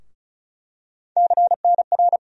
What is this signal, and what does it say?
Morse code, heard as a keyed tone, about 36 wpm, 700 Hz. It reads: CNR
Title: USA Call Sign Postfixes - 3 Letters 35wpm